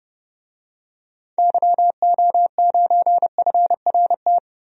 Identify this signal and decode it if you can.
Morse code, heard as a keyed tone, about 29 wpm, 700 Hz. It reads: YO9FRT
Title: DXCC-100 Call Signs - 6 Characters - Repeated 3x - 30wpm